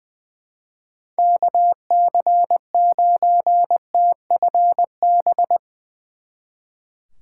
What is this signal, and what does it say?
Morse code, heard as a keyed tone, about 20 wpm, 700 Hz. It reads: KC9TFB
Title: Speed Racing - USA Call Signs 20wpm